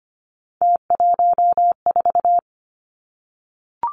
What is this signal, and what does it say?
Morse code, heard as a keyed tone, about 25 wpm, 700 Hz. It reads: T14
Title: Three Challenging Letter-Number Combinations 25wpm